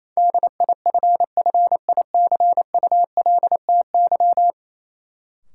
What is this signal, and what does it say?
Morse code, heard as a keyed tone, about 28 wpm, 700 Hz. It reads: DIFFICULTY
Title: Sentences from Top 1500 Words - New Word Drill - 28wpm